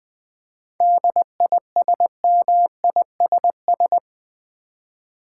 Morse code, heard as a keyed tone, about 20 wpm, 700 Hz. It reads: DISMISS
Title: 7 Letter Words - First 3 Letters Given - 20wpm